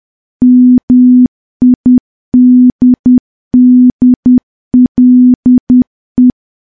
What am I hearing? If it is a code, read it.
Morse code, heard as a keyed tone, about 10 wpm, 250 Hz. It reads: MIDDLE
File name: morse / morse-training-10WPM